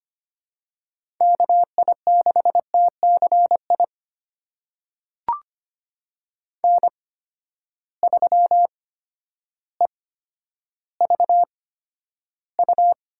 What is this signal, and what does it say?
Morse code, heard as a keyed tone, about 25 wpm, spaced as for 11 wpm, 700 Hz. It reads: KI6TCI N3EVU
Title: USA Call Signs - 8x Character Spacing - 25wpm